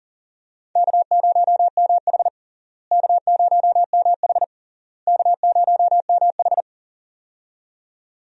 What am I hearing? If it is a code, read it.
Morse code, heard as a keyed tone, about 40 wpm, 700 Hz. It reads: K0MH K0MH K0MH
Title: USA Call Signs - 4 Characters - Repeated 3x - 40wpm